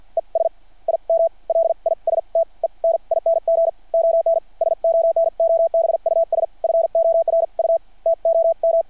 Morse code, heard as a keyed tone, about 53 wpm, spaced as for 42 wpm, 660 Hz. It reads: ES IM LISTENING ON SONOBUS FOR U TO M
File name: archived / CRSnet